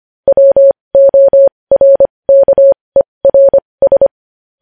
Morse code, heard as a keyed tone, about 25 wpm, spaced as for 20 wpm, 550 Hz. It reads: WORKERS
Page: news-headlines-in-morse-code-at-20-wpm